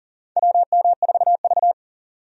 Morse code, heard as a keyed tone, about 42 wpm, 690 Hz. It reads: WM4V
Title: USA Call Signs - 4 Characters - 40wpm